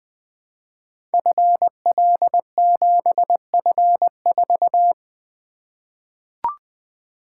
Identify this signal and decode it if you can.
Morse code, heard as a keyed tone, about 20 wpm, 700 Hz. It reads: FL7F4
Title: Code Groups - 5 Letter:Numbers - 20wpm